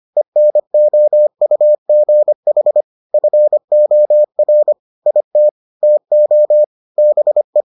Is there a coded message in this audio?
Yes: Morse code from 0.2 to 7.6 s, heard as a keyed tone, about 25 wpm, 600 Hz. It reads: ENOUGH FOR IT TO BE